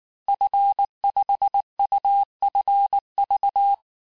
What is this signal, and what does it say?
Morse code, heard as a keyed tone, about 19 wpm, 800 Hz. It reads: F5UFV
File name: cours-du-lundi files / cours-1595--FIN-de-QSO-vitesse-18mpm